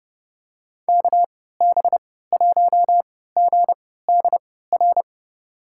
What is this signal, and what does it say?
Morse code, heard as a keyed tone, about 30 wpm, spaced as for 17 wpm, 700 Hz. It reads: KB1GDR
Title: USA Call Signs - 6 Characters - 3x Character Spacing - 30wpm